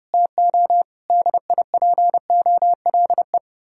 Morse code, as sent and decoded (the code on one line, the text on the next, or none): - --- / -.. .. .--. --- .-.. .
TO DIPOLE